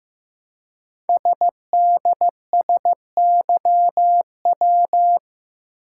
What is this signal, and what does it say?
Morse code, heard as a keyed tone, about 15 wpm, 700 Hz. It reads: SDSYW
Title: Code Groups - 5 Letter:Numbers - 15wpm